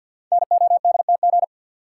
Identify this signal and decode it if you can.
Morse code, heard as a keyed tone, about 50 wpm, 700 Hz. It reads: NODTG
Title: Code Groups - 5 Letters - 50wpm